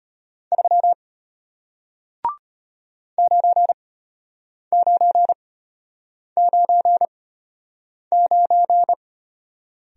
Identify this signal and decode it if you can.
Morse code, heard as a keyed tone, about 33 wpm, spaced as for 8 wpm, 700 Hz. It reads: 3 9999